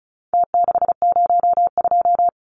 Morse code, heard as a keyed tone, about 34 wpm, 700 Hz. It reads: T602
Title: Numbers - Contest Style - 4-Digit 35wpm